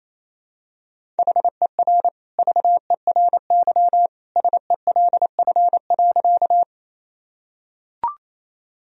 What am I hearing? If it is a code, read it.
Morse code, heard as a keyed tone, about 28 wpm, 700 Hz. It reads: HER VERY SELF.
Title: Sentences from Top 1000 Words - Review Part 1 - 28wpm